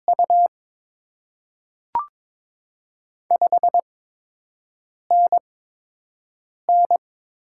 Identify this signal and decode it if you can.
Morse code, heard as a keyed tone, about 22 wpm, spaced as for 6 wpm, 700 Hz. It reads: U 5NN
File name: Top 20 QSO Elements - 8x Character Spacing - 22wpm